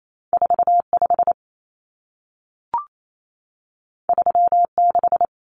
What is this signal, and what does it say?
Morse code, heard as a keyed tone, about 28 wpm, 700 Hz. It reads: 45 36
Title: Common Cut Numbers - 2-Digit 28wpm